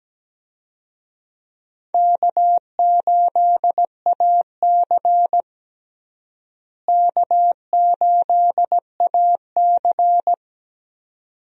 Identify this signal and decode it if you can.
Morse code, heard as a keyed tone, about 17 wpm, 700 Hz. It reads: K8AC K8AC
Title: USA Call Signs - 4 Characters - Repeated 2x - 17wpm